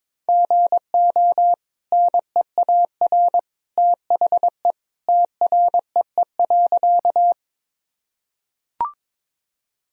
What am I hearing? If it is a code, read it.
Morse code, heard as a keyed tone, about 22 wpm, 700 Hz. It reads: GO NEAR THE TREE.
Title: Sentences from Top 1000 Words - Review Part 1 - 22wpm